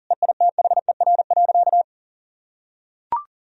Morse code, heard as a keyed tone, about 40 wpm, 700 Hz. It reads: EITHER.
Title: Sentences from Top 1000 Words - Review Part 4 - 40wpm